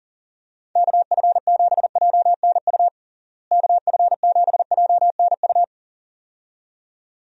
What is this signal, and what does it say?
Morse code, heard as a keyed tone, about 37 wpm, 700 Hz. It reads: KF7JNU KF7JNU
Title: USA Call Signs - 6 Characters - Repeated 2x - 40wpm